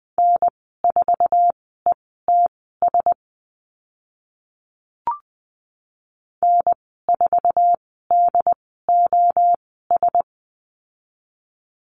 Morse code, heard as a keyed tone, about 20 wpm, spaced as for 14 wpm, 700 Hz. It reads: N4ETS N4DOS
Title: 069 Lesson 12g - Letter d - Call Signs 20wpm